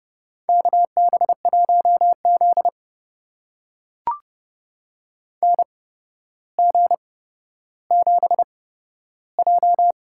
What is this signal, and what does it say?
Morse code, heard as a keyed tone, about 31 wpm, spaced as for 13 wpm, 700 Hz. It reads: KB1Z NG7J
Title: USA Call Signs - 8x Character Spacing - 30wpm